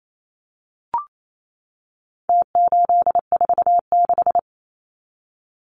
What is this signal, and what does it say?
Morse code, heard as a keyed tone, about 28 wpm, 700 Hz. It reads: T846
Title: Numbers - Contest Style - 4-Digit 28wpm